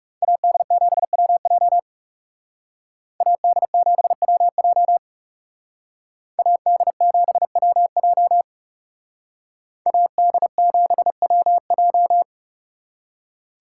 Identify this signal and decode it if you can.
Morse code, heard as a keyed tone, about 38 wpm, 700 Hz. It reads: AD7WJ AD7WJ AD7WJ AD7WJ